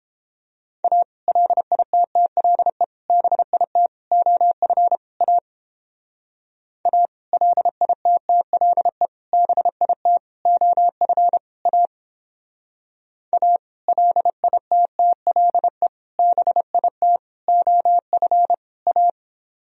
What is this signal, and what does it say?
Morse code, heard as a keyed tone, about 31 wpm, 700 Hz. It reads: A LITTLE BIT OF A A LITTLE BIT OF A A LITTLE BIT OF A